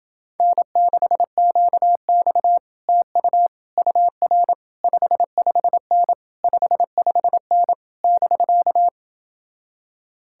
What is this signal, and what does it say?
Morse code, heard as a keyed tone, about 28 wpm, 700 Hz. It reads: N6QX TU UR 55N 55N <BK>